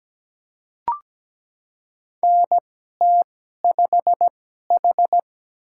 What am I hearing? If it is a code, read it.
Morse code, heard as a keyed tone, about 17 wpm, spaced as for 12 wpm, 700 Hz. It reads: NT5H
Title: USA Call Signs - 2x Character Spacing - 17wpm